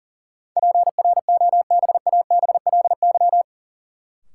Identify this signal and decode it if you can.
Morse code, heard as a keyed tone, about 40 wpm, 700 Hz. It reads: PROBABLY
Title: Sentences from Top 700 Words - New Word Drill - 40wpm